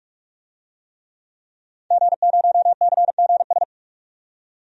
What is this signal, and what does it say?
Morse code, heard as a keyed tone, about 44 wpm, 690 Hz. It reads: G0CGS
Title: DXCC-100 Call Signs - 5 Characters - 45wpm